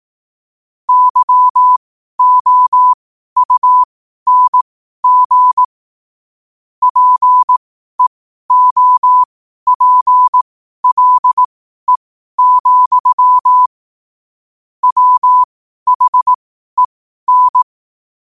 Morse code, heard as a keyed tone, about 18 wpm, spaced as for 12 wpm, 1000 Hz. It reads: YOUNG PEOPLE, WHEN
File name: qotdfiles / 12wpm